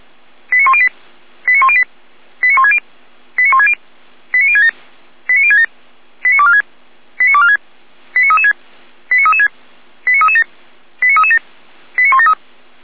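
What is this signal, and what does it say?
Five-tone selective call (ZVEI), heard as quick runs of five tones, tones of about 70 ms. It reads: 89108 89108 89160 89160 89067 89067 89246 89246 89336 89336 89337 89337 89173
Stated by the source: ZVEI five-tone selective call received on SDR.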